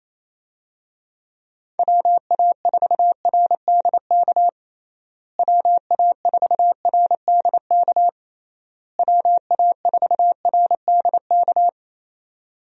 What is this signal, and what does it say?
Morse code, heard as a keyed tone, about 29 wpm, 700 Hz. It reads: WA4RDK WA4RDK WA4RDK
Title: USA Call Signs - 6 Characters - Repeated 3x - 28wpm